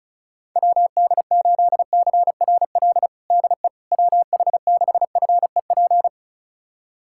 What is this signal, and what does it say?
Morse code, heard as a keyed tone, about 36 wpm, 700 Hz. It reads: WD8CRL DE WH6FEP
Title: USA Callsign DE Callsign - 6 Characters - 35wpm